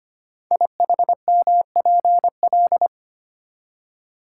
Morse code, heard as a keyed tone, about 25 wpm, 700 Hz. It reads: IHMPL
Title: Code Groups - 5 Letters - 25wpm